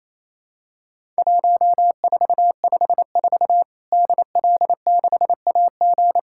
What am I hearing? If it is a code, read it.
Morse code, heard as a keyed tone, about 28 wpm, 700 Hz. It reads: 1454 DL6AG
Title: Worked All Europe DX Contest - QTC Exchange - 28wpm